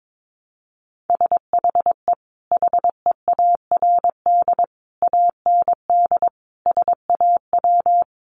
Morse code, heard as a keyed tone, about 22 wpm, 700 Hz. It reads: SHE HEARD AND SAW
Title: Sentences from Top 500 Words 22wpm